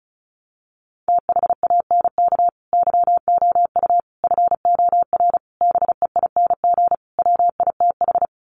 Morse code, heard as a keyed tone, about 34 wpm, 700 Hz. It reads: THANK YOU FOR BEING WITH
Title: Sets of 5 Words 35wpm